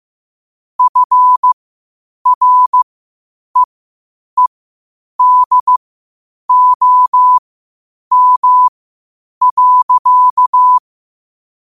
Morse code, heard as a keyed tone, about 15 wpm, spaced as for 8 wpm, 1000 Hz. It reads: FREEDOM.
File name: qotdfiles / 8wpm